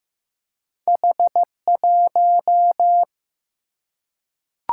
Morse code, heard as a keyed tone, about 15 wpm, 700 Hz. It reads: H1
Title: Single-Letter Single-Number 15wpm